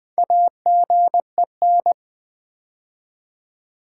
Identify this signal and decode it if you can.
Morse code, heard as a keyed tone, about 20 wpm, 700 Hz. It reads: AGEN
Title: Build A Word - 2 Letters At A Time - 6-Letter Words - 20wpm